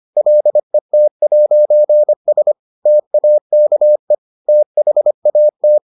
Morse code, heard as a keyed tone, about 25 wpm, 600 Hz. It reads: LET'S TAKE THAT